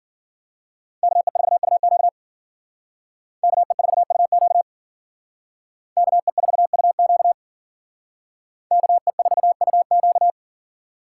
Morse code, heard as a keyed tone, about 51 wpm, 700 Hz. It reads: KE4UQ KE4UQ KE4UQ KE4UQ